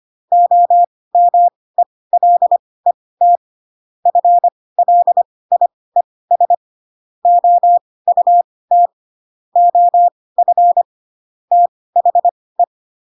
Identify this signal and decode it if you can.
Morse code, heard as a keyed tone, about 25 wpm, spaced as for 18 wpm, 700 Hz. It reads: OMELET FLIES OUT OF THE
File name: Pinocchio / Pinocchio ch5 12 25 ik1hge